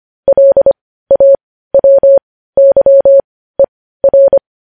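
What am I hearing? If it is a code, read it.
Morse code, heard as a keyed tone, about 25 wpm, spaced as for 15 wpm, 550 Hz. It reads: LAWYER
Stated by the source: News headlines in Morse code at 15 words per minute, characters sent at 25 WPM.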